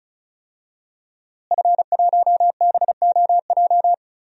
Morse code, heard as a keyed tone, about 36 wpm, 700 Hz. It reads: F1BOJ
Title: DXCC-100 Call Signs - 5 Characters - Repeated 3x - 35wpm